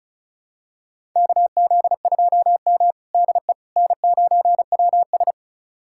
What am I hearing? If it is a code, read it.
Morse code, heard as a keyed tone, about 36 wpm, 700 Hz. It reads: KZ2M DE N9WS
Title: USA Callsign DE Callsign - 4 Characters - 35wpm